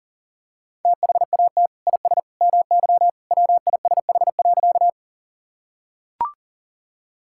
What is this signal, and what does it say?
Morse code, heard as a keyed tone, about 40 wpm, 700 Hz. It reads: THAT IS MY WISH.
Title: Sentences from Top 1600 Words 40wpm